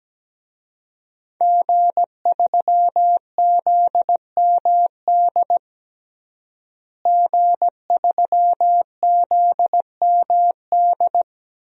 Morse code, heard as a keyed tone, about 17 wpm, 700 Hz. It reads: G3ZMD G3ZMD
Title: DXCC-100 Call Signs - 5 Characters - Repeated 2x - 17wpm